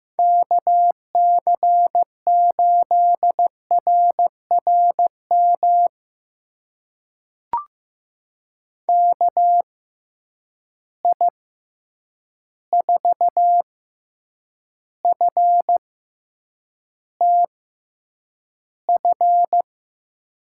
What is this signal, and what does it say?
Morse code, heard as a keyed tone, about 15 wpm, spaced as for 8 wpm, 700 Hz. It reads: KC8RRM KI4FTF
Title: USA Call Signs - 6 Characters - 6x Character Spacing - 15wpm